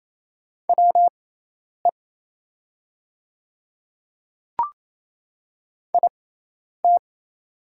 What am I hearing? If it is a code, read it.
Morse code, heard as a keyed tone, about 27 wpm, spaced as for 10 wpm, 700 Hz. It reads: WE IT
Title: 6x Character Spacing - Top 100 Words - 28wpm